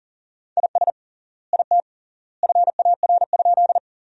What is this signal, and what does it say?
Morse code, heard as a keyed tone, about 40 wpm, 700 Hz. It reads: IS IT FAR?
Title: 3x Word-Spacing - Sentences from Top 300 Words - 40wpm